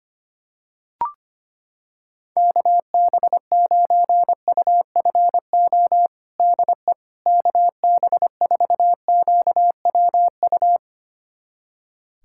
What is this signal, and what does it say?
Morse code, heard as a keyed tone, about 25 wpm, 700 Hz. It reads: KB9UFO DE KB4QWU
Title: USA Callsign DE Callsign - 6 Characters - 25wpm